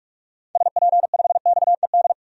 Morse code, heard as a keyed tone, about 41 wpm, 700 Hz. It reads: SP5XED
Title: DXCC-100 Call Signs - 6 Characters - 45wpm